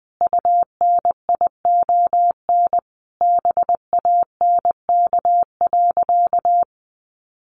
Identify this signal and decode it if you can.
Morse code, heard as a keyed tone, about 20 wpm, 700 Hz. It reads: UNION BANK.
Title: Sentences from Top 1400 Words 20wpm